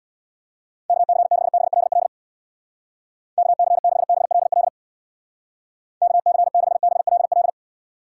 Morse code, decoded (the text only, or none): KC6BLL KC6BLL KC6BLL